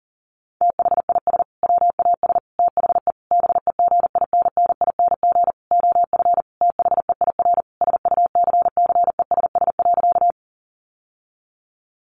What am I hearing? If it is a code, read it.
Morse code, heard as a keyed tone, about 40 wpm, 700 Hz. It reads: THIS WAS THE BEGINNING OF THEIR SUCCESS.
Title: Sentences from Top 900 Words 40wpm